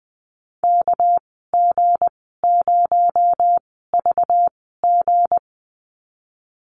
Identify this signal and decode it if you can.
Morse code, heard as a keyed tone, about 20 wpm, spaced as for 14 wpm, 700 Hz. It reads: KG0VG
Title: 245 Lesson 39g - Letter z - Call Signs 20wpm